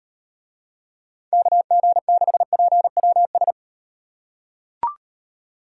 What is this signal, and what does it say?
Morse code, heard as a keyed tone, about 38 wpm, 700 Hz. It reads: KG6PWS